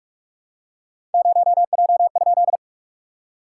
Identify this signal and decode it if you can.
Morse code, heard as a keyed tone, about 43 wpm, 690 Hz. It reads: 0J?
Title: SOTA QSO Practice 45wpm